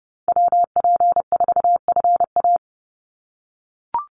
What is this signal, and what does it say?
Morse code, heard as a keyed tone, about 30 wpm, 700 Hz. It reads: WP4FA
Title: USA Call Signs 30wpm